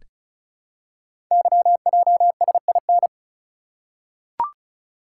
Morse code, heard as a keyed tone, about 36 wpm, 700 Hz. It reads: YJSIN
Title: Code Groups - 5 Letters - 35wpm